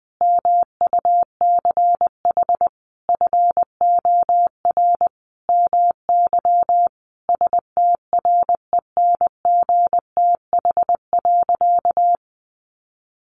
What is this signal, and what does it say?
Morse code, heard as a keyed tone, about 20 wpm, 700 Hz. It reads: MUCH FOR MY STRENGTH.
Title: Sentences from Top 1300 Words 20wpm